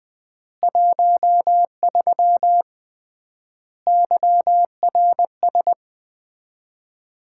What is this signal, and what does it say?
Morse code, heard as a keyed tone, about 20 wpm, 700 Hz. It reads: 13 YRS
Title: Common QSO Phrases - 3x Word-Spacing - 20wpm